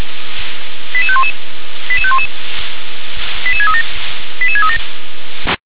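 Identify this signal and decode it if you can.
Five-tone selective call (ZVEI), heard as quick runs of five tones, tones of about 70 ms. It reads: 88511 88511 88637 88637
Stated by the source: ZVEI five-tone call received by SDR.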